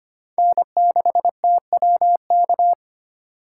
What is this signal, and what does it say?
Morse code, heard as a keyed tone, about 31 wpm, 700 Hz. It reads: N6TWK
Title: Speed Racing - USA 1x3 Call Signs 25wpm